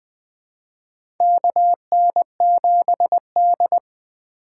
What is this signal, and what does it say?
Morse code, heard as a keyed tone, about 20 wpm, 700 Hz. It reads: KN7D
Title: 2023 POTA Hunter Call Signs - 20wpm